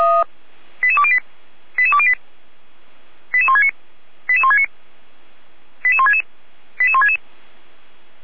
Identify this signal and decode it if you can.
Five-tone selective call (ZVEI), heard as quick runs of five tones, tones of about 70 ms. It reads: 88208 88208 88169 88169 88166 88166
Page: /455/SDR/ZVEI/88199/